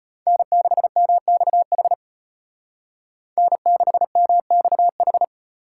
Morse code, heard as a keyed tone, about 32 wpm, 700 Hz. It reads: N6MXH N6MXH